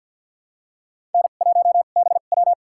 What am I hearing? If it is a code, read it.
Morse code, heard as a keyed tone, about 47 wpm, spaced as for 35 wpm, 700 Hz. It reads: N1BW
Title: USA Call Signs - 2x Character Spacing - 50wpm